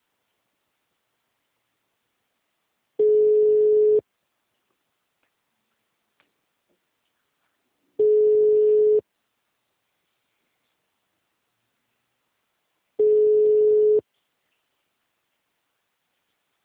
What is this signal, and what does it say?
Ringback tone (Europe, 425 Hz).